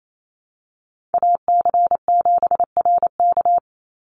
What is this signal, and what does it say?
Morse code, heard as a keyed tone, about 28 wpm, 700 Hz. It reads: AC7RK
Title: USA Call Signs 28wpm